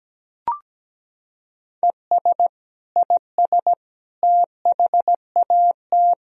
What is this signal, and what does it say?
Morse code, heard as a keyed tone, about 17 wpm, 700 Hz. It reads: ES IS THAT
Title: Sentences from Top 100 Words and Abbreviations 17wpm